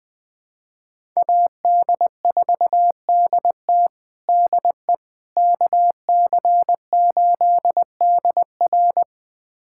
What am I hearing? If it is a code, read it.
Morse code, heard as a keyed tone, about 21 wpm, 700 Hz. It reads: AD4DT DE KC8DR
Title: USA Callsign DE Callsign - 5 Characters - 20wpm